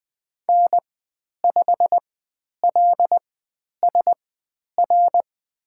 Morse code, heard as a keyed tone, about 20 wpm, spaced as for 10 wpm, 700 Hz. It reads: N5LSR
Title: 095 Lesson 16g - Letter c - Call Signs 20wpm